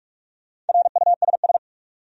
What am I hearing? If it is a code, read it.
Morse code, heard as a keyed tone, about 46 wpm, 700 Hz. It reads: RUSS
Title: USA Common Names - 45wpm